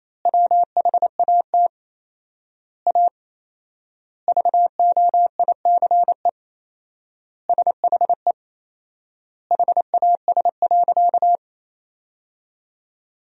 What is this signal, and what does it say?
Morse code, heard as a keyed tone, about 27 wpm, 700 Hz. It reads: WHAT A VOICE SHE HAS.
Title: 4x Word-Spacing - Sentences from Top 500 Words - 28wpm